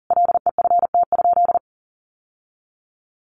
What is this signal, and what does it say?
Morse code, heard as a keyed tone, about 40 wpm, 700 Hz. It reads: LEFT?
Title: Sentences from Top 1200 Words 40wpm